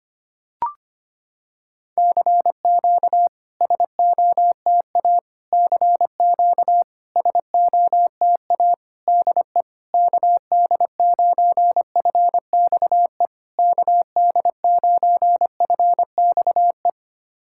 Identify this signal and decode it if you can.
Morse code, heard as a keyed tone, about 25 wpm, 690 Hz. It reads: CQ SOTA CQ SOTA DE KD9FXE KD9FXE
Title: SOTA QSO Practice 25wpm